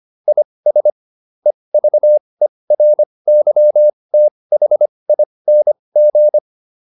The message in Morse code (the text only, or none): IS EVERYTHING